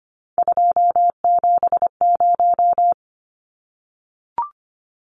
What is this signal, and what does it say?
Morse code, heard as a keyed tone, about 25 wpm, 690 Hz. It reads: 270
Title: Numbers - 3-Digit 25wpm